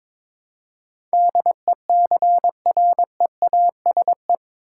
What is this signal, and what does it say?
Morse code, heard as a keyed tone, about 22 wpm, 700 Hz. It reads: DECREASE
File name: practice / 22wpm / Eight Letter Words - 22wpm